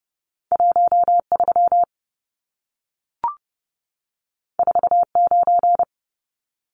Morse code, heard as a keyed tone, about 30 wpm, 700 Hz. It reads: 13 49
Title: Numbers - 2-Digit 30wpm